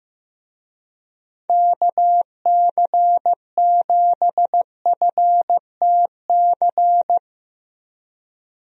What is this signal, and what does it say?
Morse code, heard as a keyed tone, about 16 wpm, 700 Hz. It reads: KC7FTC
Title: USA Call Signs - 6 Characters - Repeated 3x - 15wpm